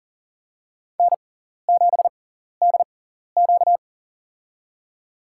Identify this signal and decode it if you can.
Morse code, heard as a keyed tone, about 39 wpm, spaced as for 14 wpm, 700 Hz. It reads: N7DQ
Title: USA Call Signs - 4 Characters - 6x Character Spacing - 40wpm